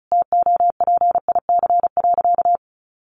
Morse code, heard as a keyed tone, about 34 wpm, 700 Hz. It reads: TOPIC.
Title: Sentences from Top 1200 Words 35wpm